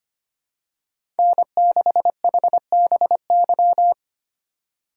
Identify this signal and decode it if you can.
Morse code, heard as a keyed tone, about 25 wpm, 700 Hz. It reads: N6HBY
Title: USA Call Signs - 3x Character Spacing - 25wpm